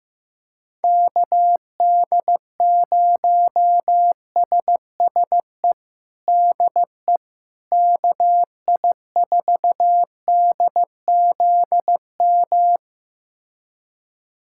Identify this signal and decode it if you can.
Morse code, heard as a keyed tone, about 15 wpm, 700 Hz. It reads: KD0SSE DE KI4DZM
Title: USA Callsign DE Callsign 15wpm